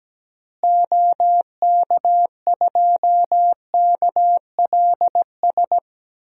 Morse code, heard as a keyed tone, about 17 wpm, 700 Hz. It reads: OK2KLS
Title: DXCC-100 Call Signs - 6 Characters - Repeated 3x - 17wpm